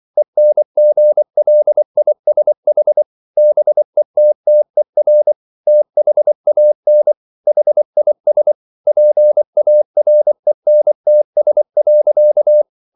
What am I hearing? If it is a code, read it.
Morse code, heard as a keyed tone, about 24 wpm, 600 Hz. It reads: ENGLISH BETTER THAN HIS PARENTS.